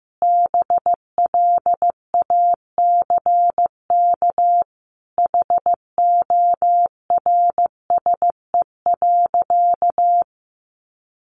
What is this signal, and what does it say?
Morse code, heard as a keyed tone, about 15 wpm, 700 Hz. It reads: BLACK HORSE.
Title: Sentences from Top 1500 Words 15wpm